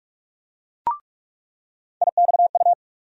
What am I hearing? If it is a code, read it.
Morse code, heard as a keyed tone, about 42 wpm, 700 Hz. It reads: IXU
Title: USA Call Sign Postfixes - 3 Letters 45wpm